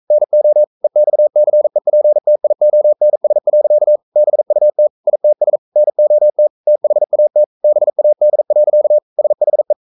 Morse code, heard as a keyed tone, about 42 wpm, 600 Hz. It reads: NO EXCEPTIONS. BUT ITS NOT THAT BAD. SHE
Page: Romanen Never af Ken Follet på engelsk.